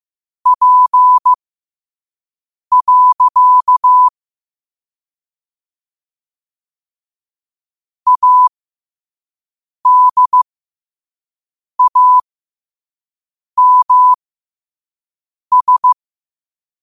Morse code, heard as a keyed tone, about 15 wpm, spaced as for 5 wpm, 1000 Hz. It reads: P. ADAMS
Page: New quotes every day in morse code at 5 Words per minute.